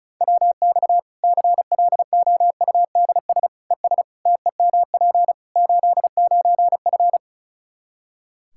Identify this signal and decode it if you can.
Morse code, heard as a keyed tone, about 34 wpm, 700 Hz. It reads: WX CLOUDS ES TEMP 89F
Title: Practice QSO - Exchange 2 - Info - 35wpm